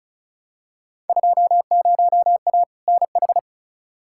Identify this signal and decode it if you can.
Morse code, heard as a keyed tone, about 36 wpm, 700 Hz. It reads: 20A NH
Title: Field Day Exchanges Encore - 35wpm